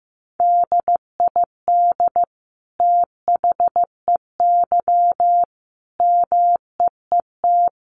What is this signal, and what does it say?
Morse code, heard as a keyed tone, about 15 wpm, 700 Hz. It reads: DID THEY MEET
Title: Sentences from Top 1100 Words 15wpm